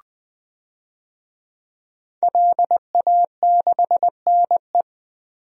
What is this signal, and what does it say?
Morse code, heard as a keyed tone, about 20 wpm, 700 Hz. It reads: LA6NE
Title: DXCC-100 Call Signs - 5 Characters - Repeated 3x - 20wpm